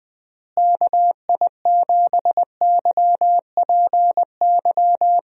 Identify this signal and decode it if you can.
Morse code, heard as a keyed tone, about 20 wpm, 700 Hz. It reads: KI7YPY
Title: USA Call Signs - 6 Characters - Repeated 2x - 20wpm